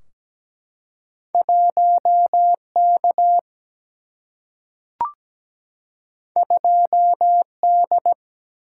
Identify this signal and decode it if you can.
Morse code, heard as a keyed tone, about 17 wpm, 700 Hz. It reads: 1K 2D
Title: Single-Letter Single-Number 17wpm